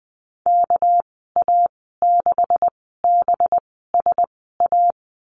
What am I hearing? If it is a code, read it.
Morse code, heard as a keyed tone, about 20 wpm, spaced as for 14 wpm, 700 Hz. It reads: KA6BSA
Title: 238 Lesson 38g - Letter q - Call Signs 20wpm